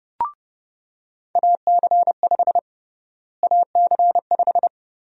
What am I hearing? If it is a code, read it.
Morse code, heard as a keyed tone, about 31 wpm, 700 Hz. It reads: AC5 AC5
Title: USA Call Sign Prefixes - Repeated 2x - 30wpm